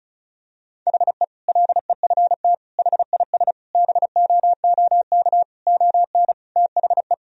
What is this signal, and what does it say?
Morse code, heard as a keyed tone, about 36 wpm, 700 Hz. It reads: HE LEFT HIS BOOK ON THE
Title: Sentences from Top 1000 Words - Review Part 2 - 35wpm